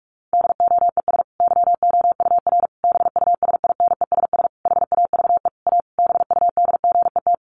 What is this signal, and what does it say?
Morse code, heard as a keyed tone, about 44 wpm, 700 Hz. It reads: DOES YOUR BUSINESS HAVE A BUDGET
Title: Sentences from Top 1200 Words 45wpm